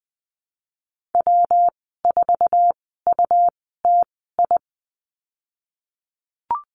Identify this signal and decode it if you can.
Morse code, heard as a keyed tone, about 20 wpm, spaced as for 15 wpm, 700 Hz. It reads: W4UTI
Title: 167 Lesson 27g - Number 7 - Call Signs 20wpm